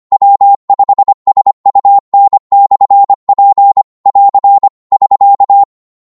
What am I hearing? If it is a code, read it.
Morse code, heard as a keyed tone, about 25 wpm, 800 Hz. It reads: W5SUN/P <AR> <SK>
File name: qso-25wpm